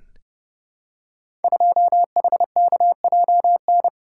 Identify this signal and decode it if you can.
Morse code, heard as a keyed tone, about 30 wpm, 700 Hz. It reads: 2HKJN
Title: Code Groups - 5 Letter:Numbers - 30wpm